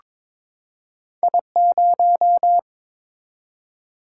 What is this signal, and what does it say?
Morse code, heard as a keyed tone, about 22 wpm, 700 Hz. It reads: I0
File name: Two Letter-Number Combinations - 22wpm